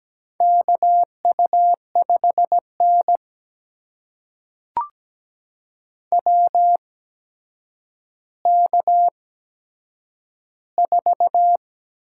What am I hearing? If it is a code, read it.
Morse code, heard as a keyed tone, about 17 wpm, spaced as for 8 wpm, 700 Hz. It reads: KU5N WK4
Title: USA Call Signs - 4 Characters - 8x Character Spacing - 17wpm